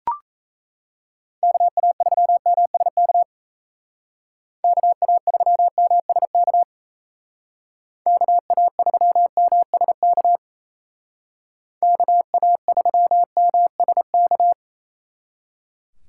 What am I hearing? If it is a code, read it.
Morse code, heard as a keyed tone, about 37 wpm, 700 Hz. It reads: KA3MSK KA3MSK KA3MSK KA3MSK